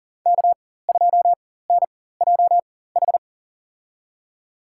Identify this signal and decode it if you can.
Morse code, heard as a keyed tone, about 40 wpm, spaced as for 19 wpm, 700 Hz. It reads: K2NJH
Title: USA Call Signs - 5 Characters - 4x Character Spacing - 40wpm